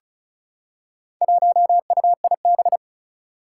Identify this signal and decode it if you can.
Morse code, heard as a keyed tone, about 36 wpm, 700 Hz. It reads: 1UIB
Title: Four Challenging Letter-Number Combinations 35wpm